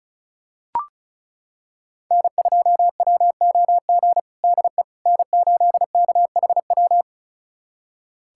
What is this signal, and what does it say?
Morse code, heard as a keyed tone, about 36 wpm, 700 Hz. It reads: N2WOG DE N8KHW
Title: USA Callsign DE Callsign - 5 Characters - 35wpm